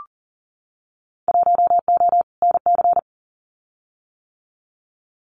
Morse code, heard as a keyed tone, about 41 wpm, 700 Hz. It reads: 1O NC
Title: Winter Field Day Exchanges Encore 40wpm